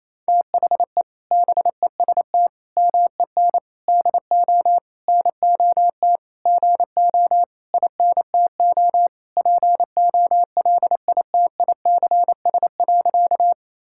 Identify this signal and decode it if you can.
Morse code, heard as a keyed tone, about 28 wpm, 700 Hz. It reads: THE BEST MEN DO NOT GO INTO POLITICS.
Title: Sentences from Top 1000 Words - Review Part 6 - 28wpm